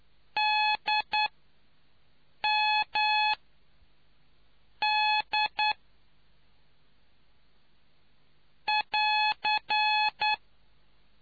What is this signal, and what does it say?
Morse code, heard as a keyed tone, about 9 wpm, spaced as for 5 wpm, 850 Hz. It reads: DMD <AR>